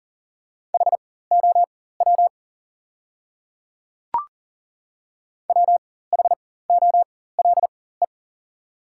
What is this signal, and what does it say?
Morse code, heard as a keyed tone, about 40 wpm, spaced as for 19 wpm, 690 Hz. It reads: HOW WHOLE
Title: 4x Character Spacing - Top 500 Words - 40wpm